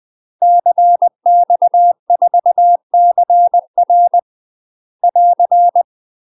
Morse code, heard as a keyed tone, about 20 wpm, 700 Hz. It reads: CX4CR <AR>
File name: UK MASTER QSOs TRAINING / 20WPM / UK MASTER QSOs 0032